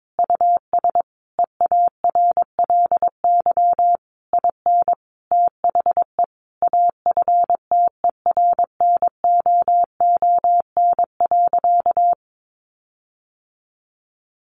Morse code, heard as a keyed tone, about 22 wpm, 700 Hz. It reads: US EARLY IN THE AFTERNOON.
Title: Sentences from Top 1500 Words 22wpm